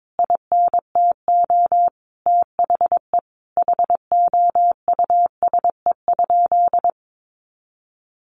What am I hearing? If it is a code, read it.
Morse code, heard as a keyed tone, about 22 wpm, 700 Hz. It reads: INTO THE HOUSE?
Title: Sentences from Top 1300 Words 22wpm